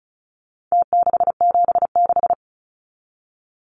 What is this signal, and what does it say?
Morse code, heard as a keyed tone, about 34 wpm, 700 Hz. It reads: T676
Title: Numbers - Contest Style - 4-Digit 35wpm